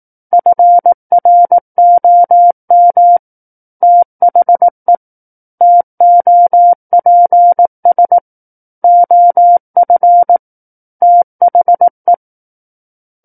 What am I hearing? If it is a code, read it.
Morse code, heard as a keyed tone, about 18 wpm, 700 Hz. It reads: FROM THE TOPS OF THE
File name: 18 WPM